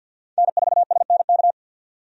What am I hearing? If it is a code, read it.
Morse code, heard as a keyed tone, about 47 wpm, 700 Hz. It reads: N4SNK
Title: USA Call Signs - 5 Characters - 50wpm